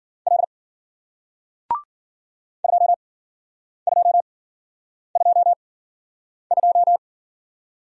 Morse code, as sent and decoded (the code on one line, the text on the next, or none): .-.. / ..--- ..--- ..--- ..---
L 2222